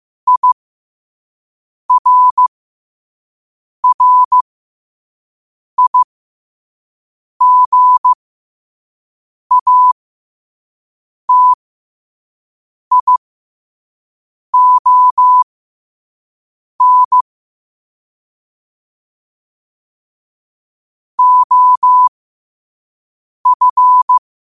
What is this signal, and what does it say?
Morse code, heard as a keyed tone, about 15 wpm, spaced as for 5 wpm, 1000 Hz. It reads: IRRIGATION OF
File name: 5wpm